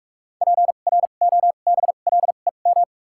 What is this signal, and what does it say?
Morse code, heard as a keyed tone, about 46 wpm, spaced as for 33 wpm, 700 Hz. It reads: PROBLEM
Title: 2x Character Spacing - Top 500 Words - 45wpm